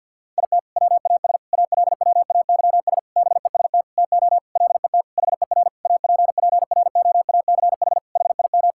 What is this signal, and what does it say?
Morse code, heard as a keyed tone, about 50 wpm, 700 Hz. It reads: IT WAS ALWAYS BEST TO LET HER APPROACH HIM
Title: Sentences from Top 1000 Words - Review Part 6 - 50wpm